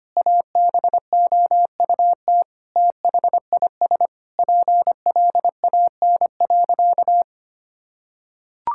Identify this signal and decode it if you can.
Morse code, heard as a keyed tone, about 25 wpm, 700 Hz. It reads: ABOUT THIS PLAN.
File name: practice / 25wpm / Sentences with 9-Word Phrases - 25wpm